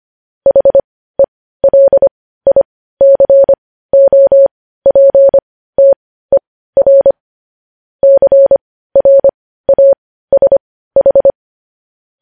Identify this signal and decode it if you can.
Morse code, heard as a keyed tone, about 25 wpm, spaced as for 15 wpm, 550 Hz. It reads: HELICOPTER CRASH
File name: podcast / News headlines-1x-25x15